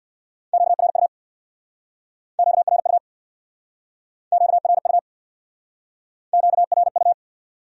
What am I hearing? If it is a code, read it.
Morse code, heard as a keyed tone, about 63 wpm, 700 Hz. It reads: QRU QRU QRU QRU